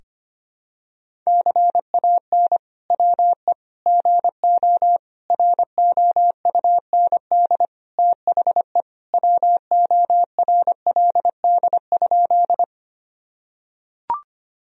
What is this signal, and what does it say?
Morse code, heard as a keyed tone, about 25 wpm, 700 Hz. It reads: CAN WE GO ROUND THE WORLD?
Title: Sentences from Top 1000 Words - Review Part 1 - 25wpm